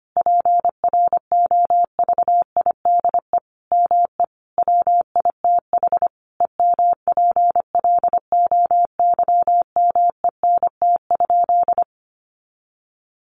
Morse code, heard as a keyed tone, about 25 wpm, 700 Hz. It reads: PROVIDE ME WITH EMPLOYMENT?
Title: Sentences from Top 1500 Words 25wpm